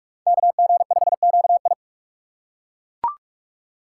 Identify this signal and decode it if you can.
Morse code, heard as a keyed tone, about 42 wpm, 700 Hz. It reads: KG5QI